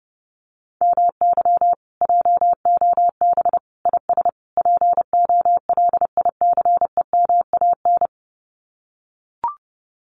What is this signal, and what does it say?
Morse code, heard as a keyed tone, about 30 wpm, 700 Hz. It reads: MY JOB IS POLICEMAN
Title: Common QSO Phrases Encore 30wpm